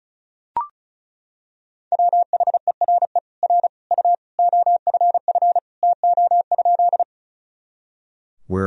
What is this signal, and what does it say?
Morse code, heard as a keyed tone, about 34 wpm, 700 Hz. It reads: WHERE R U OFF TO?
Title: Sentences from Top 200 Words and Abbreviations 35wpm